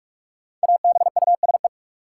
Morse code, heard as a keyed tone, about 41 wpm, 700 Hz. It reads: ABUSE
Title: Sentences from Top 1300 Words - New Word Drill - 45wpm